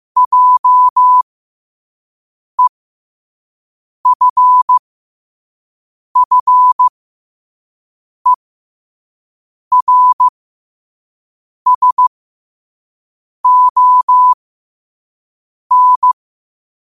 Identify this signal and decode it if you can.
Morse code, heard as a keyed tone, about 15 wpm, spaced as for 5 wpm, 1000 Hz. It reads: JEFFERSON
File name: qotdfiles / 5wpm